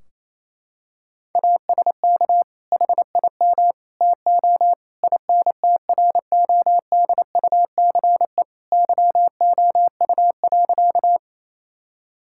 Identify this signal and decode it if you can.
Morse code, heard as a keyed tone, about 28 wpm, 700 Hz. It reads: ASK HIM TO INTRODUCE YOU.
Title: Sentences from Top 1600 Words 28wpm